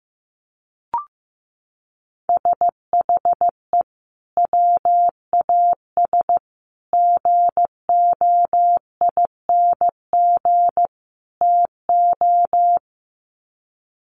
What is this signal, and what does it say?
Morse code, heard as a keyed tone, about 15 wpm, 700 Hz. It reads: SHE WAS GOING TO
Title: Sets of 4 Words 15wpm